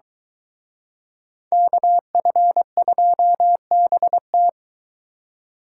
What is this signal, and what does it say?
Morse code, heard as a keyed tone, about 22 wpm, 700 Hz. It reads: KF2BT